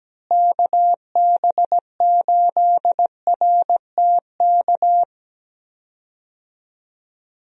Morse code, heard as a keyed tone, about 17 wpm, 700 Hz. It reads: KB8RTK
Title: USA Call Signs - 6 Characters - Repeated 2x - 17wpm